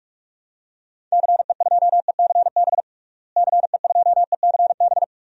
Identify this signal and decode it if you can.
Morse code, heard as a keyed tone, about 44 wpm, 700 Hz. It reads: CE2ECB CE2ECB
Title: DXCC-100 Call Signs - 6 Characters - Repeated 3x - 45wpm